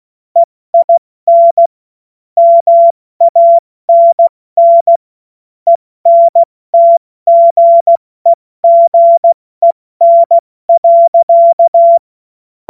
Morse code, heard as a keyed tone, about 16 wpm, 680 Hz. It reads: EIN MANN ENTGEGEN.
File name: Morsekurs / Witze / Witz 31.Oktober 2023 70 F80